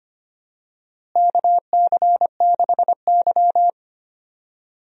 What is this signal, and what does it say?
Morse code, heard as a keyed tone, about 25 wpm, 700 Hz. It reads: KC6Y
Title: USA Call Signs - 4 Characters - Repeated 3x - 25wpm